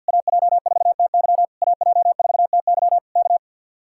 Morse code, heard as a keyed tone, about 62 wpm, 700 Hz. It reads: AJ4TY AJ4TY K
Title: Speed-Racing Common QSO Phrases - 50wpm